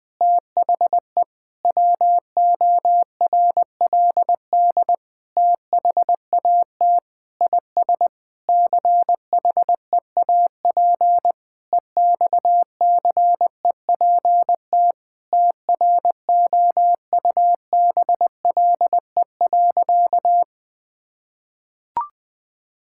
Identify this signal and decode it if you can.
Morse code, heard as a keyed tone, about 20 wpm, 700 Hz. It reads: THE WORLD THAT IS CHEAP EXCEPT TROUBLE.
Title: Sentences from Top 1600 Words 20wpm